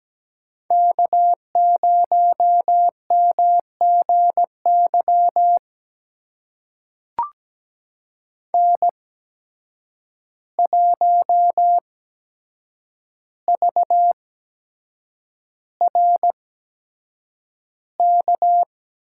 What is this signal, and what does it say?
Morse code, heard as a keyed tone, about 17 wpm, spaced as for 7 wpm, 700 Hz. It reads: K0MGY N1VRK
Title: USA Call Signs - 5 Characters - 8x Character Spacing - 17wpm